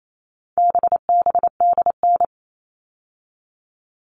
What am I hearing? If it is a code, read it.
Morse code, heard as a keyed tone, about 28 wpm, 700 Hz. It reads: BBDN
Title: All Cut Numbers - 4-Digit 28wpm